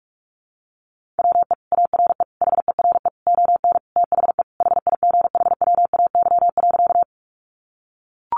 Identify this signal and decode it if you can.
Morse code, heard as a keyed tone, about 44 wpm, 700 Hz. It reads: WE ARE HERE ON THE HIGHWAY.
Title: Sentences from Top 1400 Words 45wpm